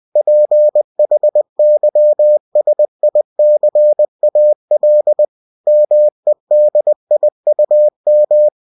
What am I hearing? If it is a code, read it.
Morse code, heard as a keyed tone, about 20 wpm, 600 Hz. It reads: PHYSICAL MEDIUM
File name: cw-books / winnie-the-pooh / winnie-20wpm-page-0117